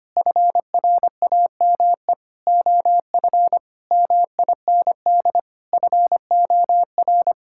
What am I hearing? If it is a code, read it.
Morse code, heard as a keyed tone, about 25 wpm, 700 Hz. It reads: FRAME OF MIND FOR
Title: Sentences from Top 1000 Words - Review Part 6 - 25wpm